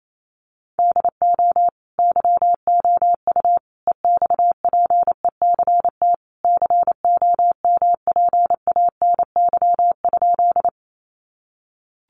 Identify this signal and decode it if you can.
Morse code, heard as a keyed tone, about 28 wpm, 700 Hz. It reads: DO YOU EXPECT COMPANY?
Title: Sentences from Top 1100 Words 28wpm